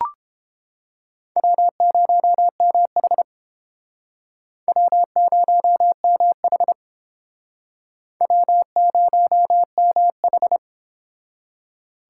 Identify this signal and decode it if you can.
Morse code, heard as a keyed tone, about 31 wpm, 700 Hz. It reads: W0MH W0MH W0MH